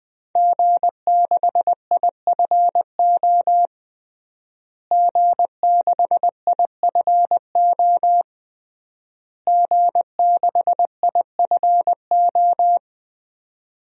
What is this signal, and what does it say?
Morse code, heard as a keyed tone, about 20 wpm, 700 Hz. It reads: G6IFO G6IFO G6IFO
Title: DXCC-100 Call Signs - 5 Characters - Repeated 3x - 20wpm